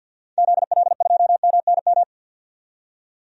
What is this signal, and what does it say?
Morse code, heard as a keyed tone, about 45 wpm, 700 Hz. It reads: ZL2MNM
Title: DXCC-100 Call Signs - 6 Characters - 50wpm